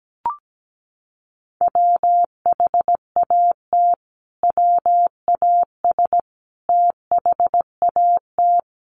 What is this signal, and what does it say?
Morse code, heard as a keyed tone, about 17 wpm, 700 Hz. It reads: WHAT WAS THAT
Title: Sentences from Top 1200 Words 17wpm